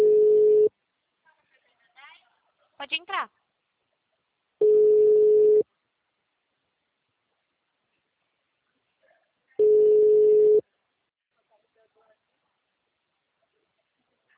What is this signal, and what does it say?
Ringback tone (Europe, 425 Hz), with speech.
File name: out- -2033- - - .39921